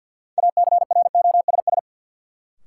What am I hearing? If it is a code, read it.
Morse code, heard as a keyed tone, about 48 wpm, 700 Hz. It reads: ACROSS
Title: Sets of 2 Words - New Word Drill - 50wpm